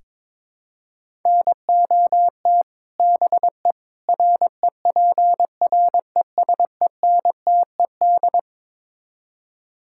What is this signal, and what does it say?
Morse code, heard as a keyed tone, about 22 wpm, 700 Hz. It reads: NOT BE REPRESENTED
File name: practice / 22wpm / Build A Phrase - 5-Word Phrases - 22wpm